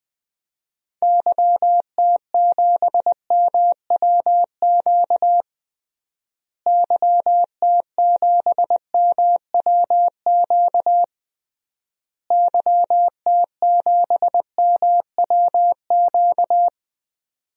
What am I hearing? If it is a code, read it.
Morse code, heard as a keyed tone, about 21 wpm, 700 Hz. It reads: YT7MWQ YT7MWQ YT7MWQ
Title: DXCC-100 Call Signs - 6 Characters - Repeated 3x - 20wpm